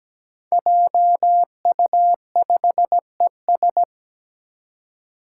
Morse code, heard as a keyed tone, about 17 wpm, 700 Hz. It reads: JU5ES
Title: Five Challenging Letter-Number Combinations 17wpm